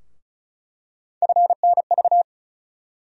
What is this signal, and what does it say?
Morse code, heard as a keyed tone, about 36 wpm, 700 Hz. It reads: FNV
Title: USA Call Sign Postfixes - 3 Letters 35wpm